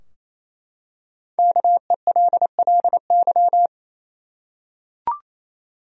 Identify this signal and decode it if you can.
Morse code, heard as a keyed tone, about 27 wpm, 700 Hz. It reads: KELLY
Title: USA Common Names - Max 5 Letters - 28wpm